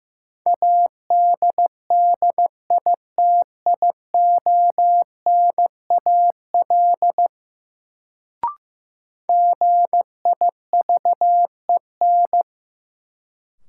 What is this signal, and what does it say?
Morse code, heard as a keyed tone, about 15 wpm, 700 Hz. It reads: ADDITIONAL GIVEN
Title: Sentences from Top 1400 Words - New Word Drill Alt - 15wpm